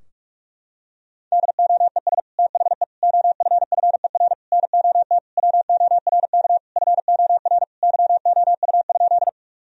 Morse code, heard as a keyed tone, about 46 wpm, 700 Hz. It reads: DOES THE OFFER NOT WORK FOR YOU?
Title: Sentences from Top 1000 Words - Review Part 4 - 45wpm